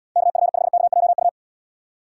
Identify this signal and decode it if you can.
Morse code, heard as a keyed tone, about 73 wpm, 700 Hz. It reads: KF5KJU